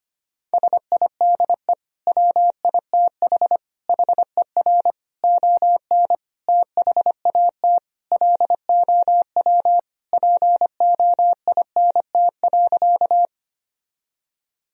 Morse code, heard as a keyed tone, about 25 wpm, 700 Hz. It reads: SIDE WITH HER ON THAT LOW POINT.
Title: Sentences from Top 1000 Words - Review Part 1 - 25wpm